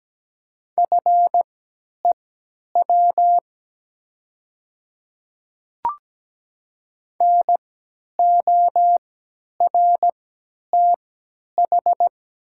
Morse code, heard as a keyed tone, about 17 wpm, spaced as for 10 wpm, 700 Hz. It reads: FEW NORTH
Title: 3x Character Spacing - Top 300 Words - 17wpm